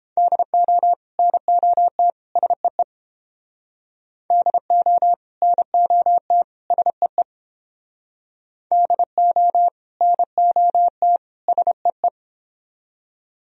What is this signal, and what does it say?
Morse code, heard as a keyed tone, about 29 wpm, 700 Hz. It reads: DO NOT SEE DO NOT SEE DO NOT SEE